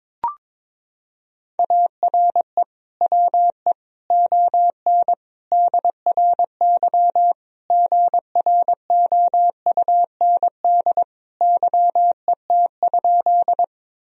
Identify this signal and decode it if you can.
Morse code, heard as a keyed tone, about 22 wpm, 700 Hz. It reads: ARE WE ON DRY GROUND YET?
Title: Sentences from Top 1000 Words - Review Part 3 - 22wpm